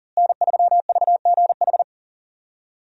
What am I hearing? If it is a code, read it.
Morse code, heard as a keyed tone, about 40 wpm, 700 Hz. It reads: N3VGH
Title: USA Call Signs - 5 Characters - Repeated 3x - 40wpm